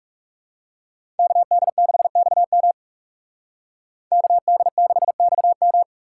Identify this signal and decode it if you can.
Morse code, heard as a keyed tone, about 39 wpm, 700 Hz. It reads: KD6XM KD6XM